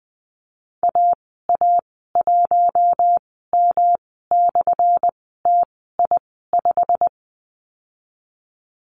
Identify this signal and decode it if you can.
Morse code, heard as a keyed tone, about 20 wpm, spaced as for 15 wpm, 700 Hz. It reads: AA1M/TI5
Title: 200 Lesson 32g - Letter k - Call Signs 20wpm